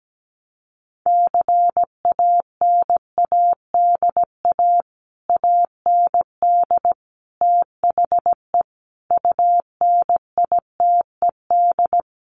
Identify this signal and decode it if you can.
Morse code, heard as a keyed tone, about 17 wpm, 690 Hz. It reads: CANADA AND THE UNITED
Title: Sets of 5 Words Encore 17wpm